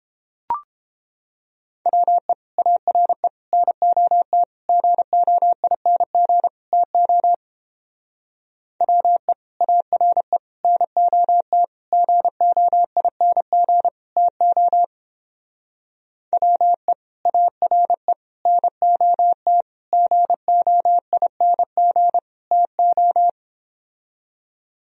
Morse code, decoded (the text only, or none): WE ARE NOT GOING TO WE ARE NOT GOING TO WE ARE NOT GOING TO